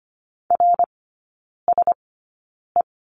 Morse code, heard as a keyed tone, about 25 wpm, spaced as for 9 wpm, 700 Hz. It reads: RSE
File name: Lesson 14e - Number 2 - Three Characters 25wpm